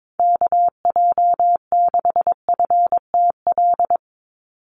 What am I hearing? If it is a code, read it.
Morse code, heard as a keyed tone, about 22 wpm, 700 Hz. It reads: KJ6FTL
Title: USA Call Signs 22wpm